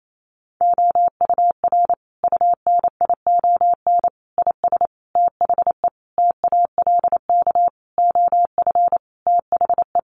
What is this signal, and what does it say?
Morse code, heard as a keyed tone, about 28 wpm, 700 Hz. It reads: OUR UNION IS THE TALK OF THE
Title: Sentences from Top 1400 Words 28wpm